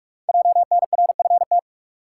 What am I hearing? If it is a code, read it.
Morse code, heard as a keyed tone, about 44 wpm, 700 Hz. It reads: JNRFT
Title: Code Groups - 5 Letter:Numbers - 45wpm